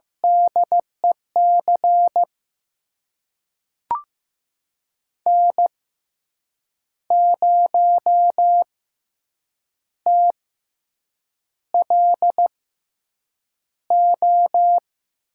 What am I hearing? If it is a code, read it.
Morse code, heard as a keyed tone, about 15 wpm, spaced as for 7 wpm, 700 Hz. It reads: DEC N0TLO
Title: USA Call Signs - 6x Character Spacing - 15wpm